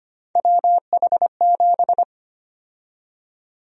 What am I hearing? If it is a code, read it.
Morse code, heard as a keyed tone, about 25 wpm, 700 Hz. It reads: WH7
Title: USA Call Sign Prefixes - Repeated 2x - 25wpm